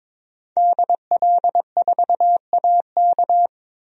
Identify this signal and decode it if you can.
Morse code, heard as a keyed tone, about 22 wpm, 700 Hz. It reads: DL4AK
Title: DXCC-100 Call Signs - 5 Characters - Repeated 2x - 22wpm